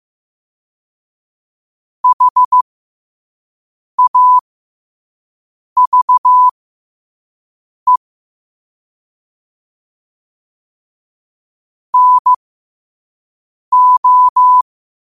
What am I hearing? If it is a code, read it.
Morse code, heard as a keyed tone, about 15 wpm, spaced as for 5 wpm, 1000 Hz. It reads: HAVE NO
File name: qotdfiles / 5wpm